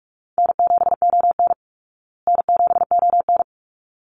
Morse code, heard as a keyed tone, about 44 wpm, 700 Hz. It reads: N7ON N7ON
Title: Call Signs Repeated 3x - 45wpm